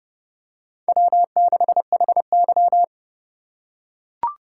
Morse code, heard as a keyed tone, about 30 wpm, 700 Hz. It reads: W6HY
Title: USA Call Signs - 4 Characters - 3x Character Spacing - 30wpm